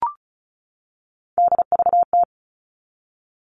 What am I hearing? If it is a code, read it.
Morse code, heard as a keyed tone, about 34 wpm, 700 Hz. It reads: DVT
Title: All Cut Numbers - 3-Digit 35wpm